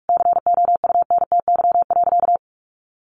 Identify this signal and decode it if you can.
Morse code, heard as a keyed tone, about 44 wpm, 700 Hz. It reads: COUNTY.
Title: Sentences from Top 1500 Words 45wpm